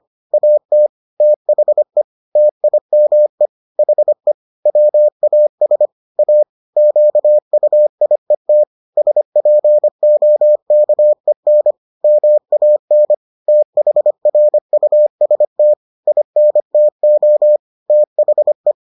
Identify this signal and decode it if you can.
Morse code, heard as a keyed tone, about 25 wpm, 600 Hz. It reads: AT THE TIME HE WAS A QUIET SPOKEN MAN THRUST INTO THE